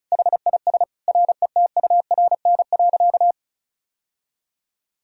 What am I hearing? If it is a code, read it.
Morse code, heard as a keyed tone, about 36 wpm, 700 Hz. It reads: HIS RETURN.
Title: Sentences from Top 1000 Words - Review Part 1 - 35wpm